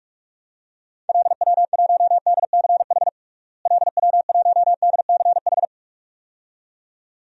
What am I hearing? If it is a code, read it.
Morse code, heard as a keyed tone, about 46 wpm, 700 Hz. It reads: LW1DCH LW1DCH
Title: DXCC-100 Call Signs - 6 Characters - Repeated 2x - 45wpm